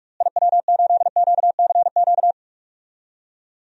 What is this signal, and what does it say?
Morse code, heard as a keyed tone, about 43 wpm, 700 Hz. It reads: IW8QCQ
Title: DXCC-100 Call Signs - 6 Characters - Repeated 3x - 45wpm